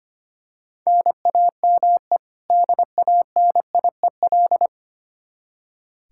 Morse code, heard as a keyed tone, about 25 wpm, 700 Hz. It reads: NAME DANIEL
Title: Practice QSO - Exchange 1 - Beginning - 25wpm